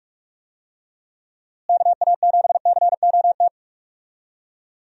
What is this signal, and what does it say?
Morse code, heard as a keyed tone, about 47 wpm, 700 Hz. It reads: KA7COT
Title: USA Call Signs - 6 Characters - Repeated 2x - 45wpm